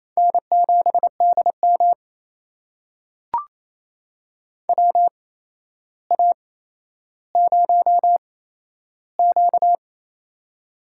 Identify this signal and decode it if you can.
Morse code, heard as a keyed tone, about 28 wpm, spaced as for 12 wpm, 700 Hz. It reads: N7DM WA0Q
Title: USA Call Signs - 4 Characters - 8x Character Spacing - 28wpm